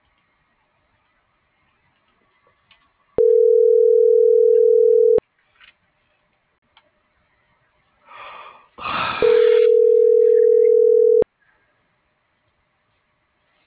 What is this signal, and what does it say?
Ringback tone (North America), 440 + 480 Hz.